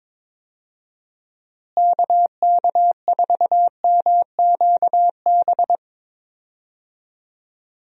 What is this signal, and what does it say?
Morse code, heard as a keyed tone, about 22 wpm, 700 Hz. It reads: KK4MQB
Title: USA Call Signs - 6 Characters - Repeated 2x - 22wpm